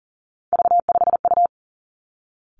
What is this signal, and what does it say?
Morse code, heard as a keyed tone, about 38 wpm, 690 Hz. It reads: V5U
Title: Three Challenging Letter-Number Combinations 40wpm